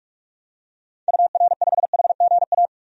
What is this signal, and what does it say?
Morse code, heard as a keyed tone, about 46 wpm, 700 Hz. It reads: UR5HGA
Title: DXCC-100 Call Signs - 6 Characters - 45wpm